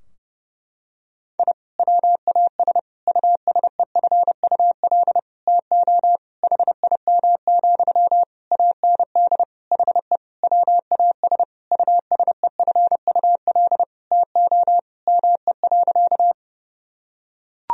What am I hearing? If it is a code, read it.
Morse code, heard as a keyed tone, about 32 wpm, 700 Hz. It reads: I WAS USEFUL TO HIM, AND HE WAS USEFUL TO ME.
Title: Sentences from Top 1000 Words - Review Part 4 - 30wpm